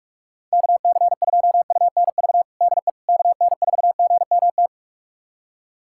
Morse code, heard as a keyed tone, about 41 wpm, 700 Hz. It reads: KC2UNV DE KN4GMT
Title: USA Callsign DE Callsign - 6 Characters - 45wpm